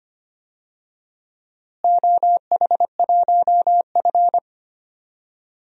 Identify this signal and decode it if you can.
Morse code, heard as a keyed tone, about 25 wpm, 700 Hz. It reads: OH1F
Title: DXCC-100 Call Signs - 4 Characters - Repeated 3x - 25wpm